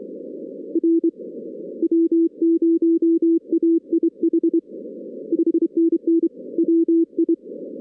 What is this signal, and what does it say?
Morse code, heard as a keyed tone, about 24 wpm, 330 Hz. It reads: R W0AIH 5NN WI
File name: 2016 arrl dx cw / 2016-ARRL-DX-CW-K5TR- - 0-22